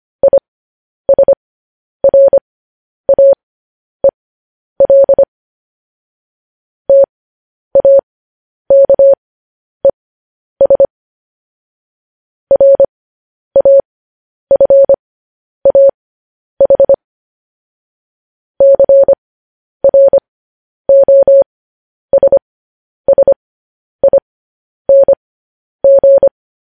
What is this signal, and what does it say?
Morse code, heard as a keyed tone, about 25 wpm, spaced as for 10 wpm, 550 Hz. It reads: ISRAEL TAKES RAFAH CROSSING